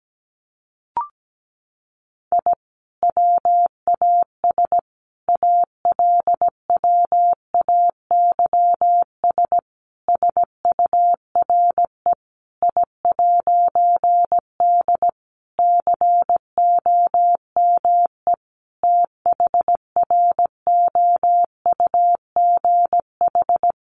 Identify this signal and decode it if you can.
Morse code, heard as a keyed tone, about 17 wpm, 700 Hz. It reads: I WAS ALWAYS SURE I'D COME THROUGH
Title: A Sound Of Thunder 17wpm